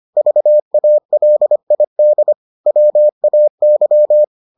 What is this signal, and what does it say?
Morse code, heard as a keyed tone, about 25 wpm, 600 Hz. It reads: VALID WAY